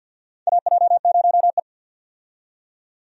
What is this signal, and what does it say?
Morse code, heard as a keyed tone, about 52 wpm, 700 Hz. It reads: AJ0E
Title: USA Call Signs - 4 Characters - 50wpm